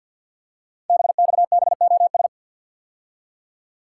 Morse code, heard as a keyed tone, about 48 wpm, 700 Hz. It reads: BXBOS
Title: Code Groups - 5 Letters - 50wpm